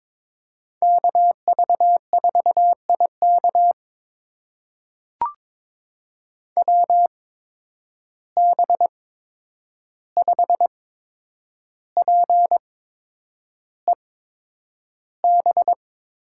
Morse code, heard as a keyed tone, about 22 wpm, spaced as for 9 wpm, 700 Hz. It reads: KV4IK WB5PEB
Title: USA Call Signs - 8x Character Spacing - 22wpm